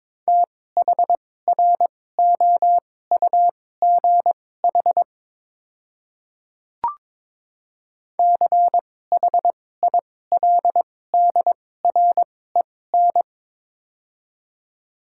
Morse code, heard as a keyed tone, about 22 wpm, spaced as for 16 wpm, 700 Hz. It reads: THROUGH CHILDREN
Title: 2x Character Spacing - Top 300 Words - 22wpm